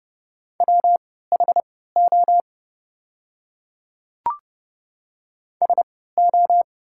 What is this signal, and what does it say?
Morse code, heard as a keyed tone, about 31 wpm, spaced as for 17 wpm, 690 Hz. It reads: WHO SO
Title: 3x Character Spacing - Top 100 Words - 30wpm